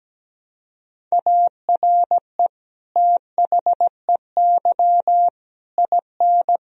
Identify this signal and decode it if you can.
Morse code, heard as a keyed tone, about 17 wpm, 700 Hz. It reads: ARE THEY IN
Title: Sentences from Top 1000 Words - Review Part 1 - 17wpm